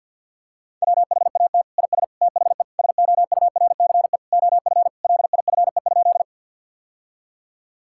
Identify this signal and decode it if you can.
Morse code, heard as a keyed tone, about 52 wpm, 700 Hz. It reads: WHAT IS THE SOURCE OF LIFE?
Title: Sentences from Top 1000 Words - Review Part 5 - 50wpm